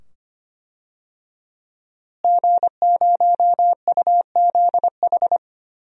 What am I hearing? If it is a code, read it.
Morse code, heard as a keyed tone, about 25 wpm, 700 Hz. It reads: G0UZH
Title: DXCC-100 Call Signs - 5 Characters - Repeated 2x - 25wpm